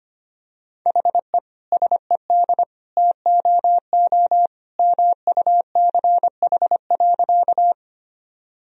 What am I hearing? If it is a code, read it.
Morse code, heard as a keyed tone, about 25 wpm, 700 Hz. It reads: HE SED TOO MUCH.
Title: Sentences from Top 200 Words and Abbreviations 25wpm